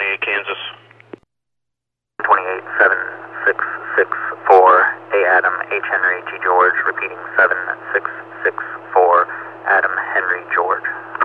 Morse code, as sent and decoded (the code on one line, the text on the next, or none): none